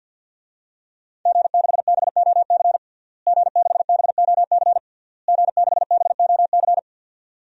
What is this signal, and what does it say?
Morse code, heard as a keyed tone, about 46 wpm, 700 Hz. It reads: G6BOC G6BOC G6BOC
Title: DXCC-100 Call Signs - 5 Characters - Repeated 3x - 50wpm